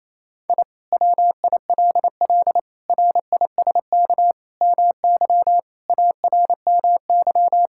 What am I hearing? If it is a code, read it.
Morse code, heard as a keyed tone, about 28 wpm, 700 Hz. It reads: I WILL RISK MY ARMY
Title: Sentences from Top 1000 Words - Review Part 5 - 28wpm